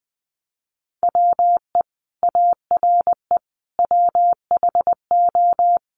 Morse code, heard as a keyed tone, about 20 wpm, 700 Hz. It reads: WE ARE WHO
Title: Sentences from Top 200 Words 20wpm